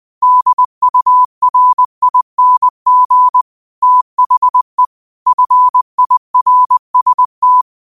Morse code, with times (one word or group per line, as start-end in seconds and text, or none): 0.2-3.4 DURING
3.8-4.8 THE
5.3-7.6 FIRST